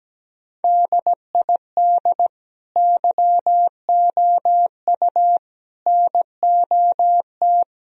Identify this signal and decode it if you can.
Morse code, heard as a keyed tone, about 17 wpm, 700 Hz. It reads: DID YOU NOT
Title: Sentences from Top 1000 Words - Review Part 4 - 17wpm